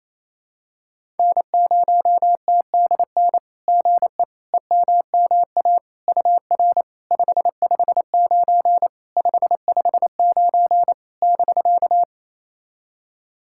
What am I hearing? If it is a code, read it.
Morse code, heard as a keyed tone, about 28 wpm, 700 Hz. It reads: N0TDN GE EMMA UR 559 559 <BK>
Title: SOTA QSO Practice 28wpm